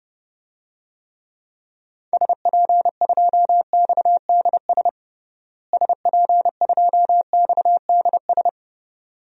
Morse code, heard as a keyed tone, about 32 wpm, 700 Hz. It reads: SP2XDS SP2XDS
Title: DXCC-100 Call Signs - 6 Characters - Repeated 2x - 30wpm